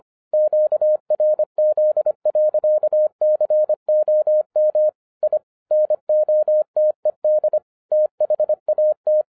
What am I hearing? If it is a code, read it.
Morse code, heard as a keyed tone, about 24 wpm, 610 Hz. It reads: QRZ.COM I NOTED THAT
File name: 2012-01-mp3 / n06-FIRST FISTS CONTACT-25wpm